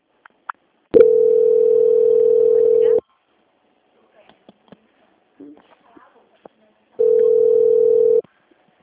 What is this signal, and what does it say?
Ringback tone (North America), 440 + 480 Hz, with speech.
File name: out- -2008- - - .0